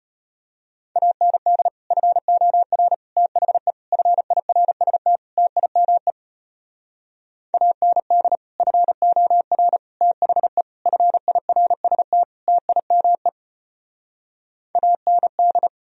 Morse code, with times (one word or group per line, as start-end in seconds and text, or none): 1.0-1.7 AND
1.9-2.9 FOR
3.2-3.7 THE
3.9-5.2 FIRST
5.4-6.1 TIME
7.5-8.4 AND
8.6-9.8 FOR
10.0-10.6 THE
10.9-12.2 FIRST
12.5-13.3 TIME
14.8-15.7 AND